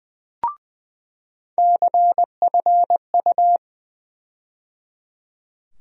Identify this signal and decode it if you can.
Morse code, heard as a keyed tone, about 20 wpm, 700 Hz. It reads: CFU
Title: USA Call Sign Postfixes - 3 Letters 20wpm